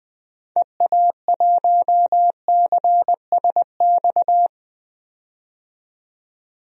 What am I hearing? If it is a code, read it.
Morse code, heard as a keyed tone, about 20 wpm, 700 Hz. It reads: EA1CSX
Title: DXCC-100 Call Signs - 6 Characters - 20wpm